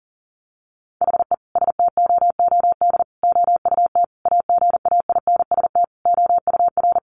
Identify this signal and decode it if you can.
Morse code, heard as a keyed tone, about 40 wpm, 700 Hz. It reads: HE STOOD OUT AGAINST OUR
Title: Sentences from Top 1500 Words 40wpm